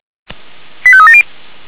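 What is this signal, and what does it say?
Five-tone selective call (ZVEI), heard as quick runs of five tones, tones of about 70 ms. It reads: 85370
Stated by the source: ZVEI five-tone call 85866 received by SDR.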